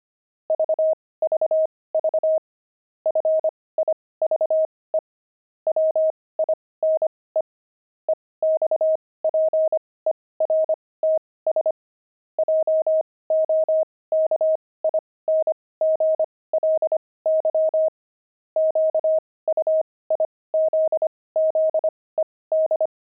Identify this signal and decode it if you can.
Morse code, heard as a keyed tone, about 25 wpm, spaced as for 18 wpm, 630 Hz. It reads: VVV FIVE WINE EXPERTS JOKINGLY QUIZZED
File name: pangram-fs-18-cs-25-wine